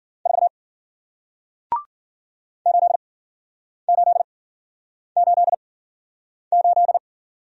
Morse code, heard as a keyed tone, about 54 wpm, 700 Hz. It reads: 4 8 8 8 8